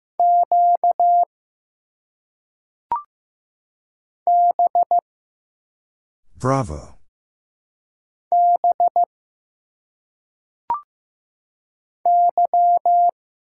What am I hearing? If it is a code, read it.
Morse code, heard as a keyed tone, about 15 wpm, 700 Hz. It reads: Q B B Y
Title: Single Letters - Phonetic Pronunciation 15wpm